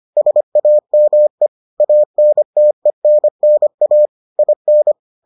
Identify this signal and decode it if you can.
Morse code, heard as a keyed tone, about 25 wpm, 600 Hz. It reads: SAME ANTENNA IN